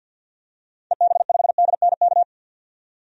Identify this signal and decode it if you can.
Morse code, heard as a keyed tone, about 53 wpm, 700 Hz. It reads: EB5DNK
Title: DXCC-100 Call Signs - 6 Characters - 50wpm